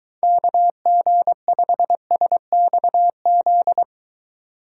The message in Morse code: KG5SXZ

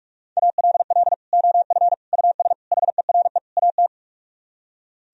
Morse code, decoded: ALL OF US HERE AT

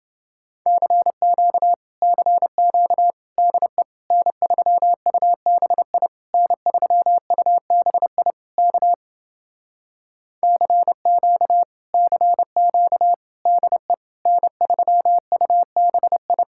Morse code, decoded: CQ CQ DE N3UBI N3UBI K CQ CQ DE N3UBI

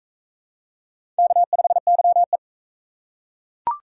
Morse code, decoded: K5YE

